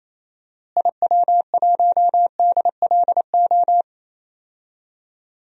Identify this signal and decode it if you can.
Morse code, heard as a keyed tone, about 27 wpm, 700 Hz. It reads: IW1DLO
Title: DXCC-100 Call Signs - 6 Characters - Repeated 2x - 28wpm